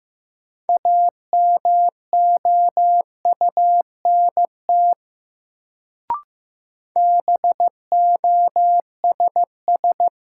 Morse code, heard as a keyed tone, about 15 wpm, 700 Hz. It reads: AMOUNT BOSS
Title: Sentences from Top 700 Words - New Word Drill - 15wpm